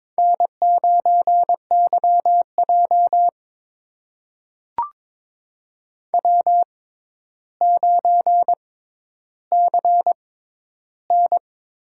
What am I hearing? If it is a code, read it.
Morse code, heard as a keyed tone, about 22 wpm, spaced as for 11 wpm, 700 Hz. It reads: N9YJ W9CN
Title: USA Call Signs - 6x Character Spacing - 22wpm